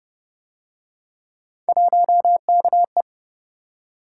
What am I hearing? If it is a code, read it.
Morse code, heard as a keyed tone, about 29 wpm, 700 Hz. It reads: 1KE